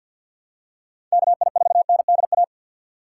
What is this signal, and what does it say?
Morse code, heard as a keyed tone, about 53 wpm, 700 Hz. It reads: KI4NDA
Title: USA Call Signs - 6 Characters - 6x Character Spacing - 50wpm